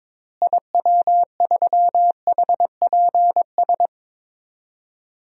Morse code, heard as a keyed tone, about 22 wpm, 700 Hz. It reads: IW3HPS
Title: DXCC-100 Call Signs - 6 Characters - Repeated 2x - 22wpm